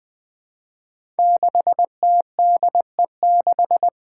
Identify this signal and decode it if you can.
Morse code, heard as a keyed tone, about 20 wpm, 700 Hz. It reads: 6TDE6
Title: Five Challenging Letter-Number Combinations 20wpm